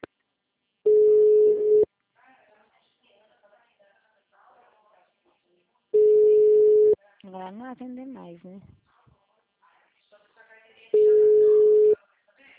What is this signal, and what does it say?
Ringback tone (Europe, 425 Hz), with speech.